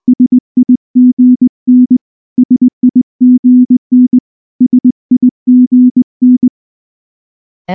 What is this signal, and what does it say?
Morse code, heard as a keyed tone, about 19 wpm, 250 Hz. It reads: SIGN SIGN SIGN
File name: morse / morse-training-20WPM-6